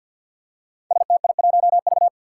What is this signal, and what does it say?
Morse code, heard as a keyed tone, about 53 wpm, 700 Hz. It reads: STI1V
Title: Five Challenging Letter-Number Combinations 50wpm